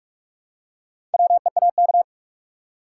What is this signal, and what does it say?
Morse code, heard as a keyed tone, about 46 wpm, 700 Hz. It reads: WEAK
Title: Four Letter Words - 45wpm